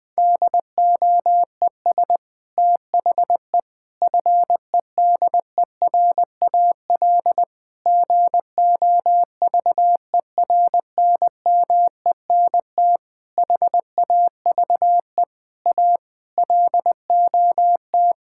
Morse code, heard as a keyed tone, about 20 wpm, 700 Hz. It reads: DOES THE FEDERAL GOVERNMENT HAVE A LOT
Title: Sentences from Top 1000 Words - Review Part 5 - 20wpm